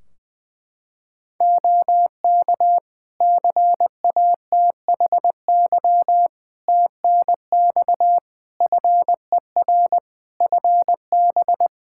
Morse code, heard as a keyed tone, about 20 wpm, 700 Hz. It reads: OK CATHY TNX FER FB
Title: Practice QSO - Exchange 4 - Ending - 20wpm